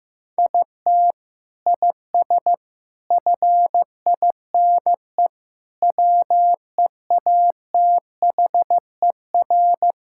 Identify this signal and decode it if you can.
Morse code, heard as a keyed tone, about 16 wpm, 700 Hz. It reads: IT IS FINE WEATHER
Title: Sentences from Top 1000 Words - Review Part 5 - 15wpm